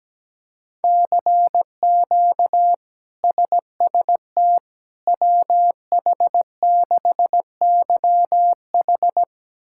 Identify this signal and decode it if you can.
Morse code, heard as a keyed tone, about 17 wpm, 700 Hz. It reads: CQ SST WH6YH
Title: K1USN SST QSO Practice - 17wpm